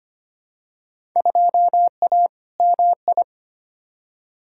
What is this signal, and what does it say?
Morse code, heard as a keyed tone, about 25 wpm, 700 Hz. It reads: 2A MI
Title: Field Day Exchanges - 25wpm